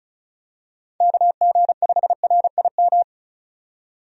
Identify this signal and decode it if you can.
Morse code, heard as a keyed tone, about 36 wpm, 700 Hz. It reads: KG5RIM
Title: USA Call Signs - 6 Characters - 35wpm